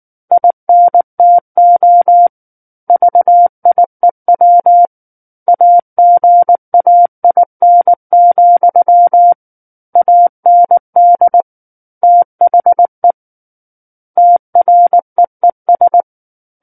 Morse code, heard as a keyed tone, about 19 wpm, 700 Hz. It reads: INTO VIEW AGAIN, AND THE TREES
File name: Book One / 10-Chapter 10 - 19 WPM